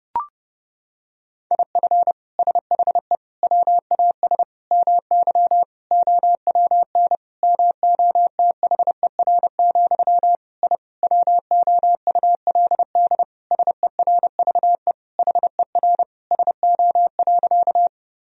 Morse code, heard as a keyed tone, about 30 wpm, 700 Hz. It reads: IF SHE WAS MY OWN MOTHER, I WOULD SERVE HER SO.
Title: Sentences from Top 1000 Words - Review Part 2 - 30wpm